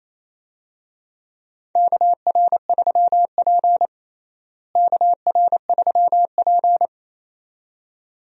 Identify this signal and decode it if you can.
Morse code, heard as a keyed tone, about 28 wpm, 700 Hz. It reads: KR3P KR3P
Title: USA Call Signs - 4 Characters - Repeated 2x - 28wpm